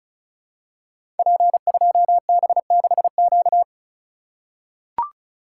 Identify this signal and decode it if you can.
Morse code, heard as a keyed tone, about 36 wpm, 700 Hz. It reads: P2B6Q
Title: Code Groups - 5 Letter:Numbers - 35wpm